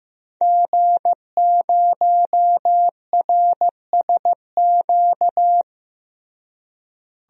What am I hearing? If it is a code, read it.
Morse code, heard as a keyed tone, about 16 wpm, 700 Hz. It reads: G0RSQ
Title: DXCC-100 Call Signs - 5 Characters - Repeated 3x - 15wpm